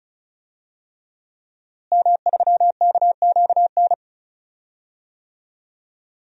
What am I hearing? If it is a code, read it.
Morse code, heard as a keyed tone, about 36 wpm, 700 Hz. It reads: M3KQN
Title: DXCC-100 Call Signs - 5 Characters - Repeated 3x - 35wpm